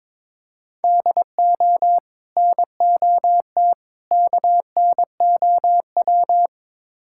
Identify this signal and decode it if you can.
Morse code, heard as a keyed tone, about 22 wpm, 700 Hz. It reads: DO NOT KNOW
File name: practice / 22wpm / Build A Phrase - 5-Word Phrases - 22wpm